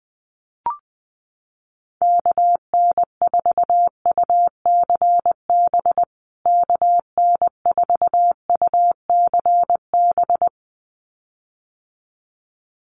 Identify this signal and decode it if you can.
Morse code, heard as a keyed tone, about 20 wpm, 700 Hz. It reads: KN4UCB KN4UCB
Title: USA Call Signs Repeated 2x 20wpm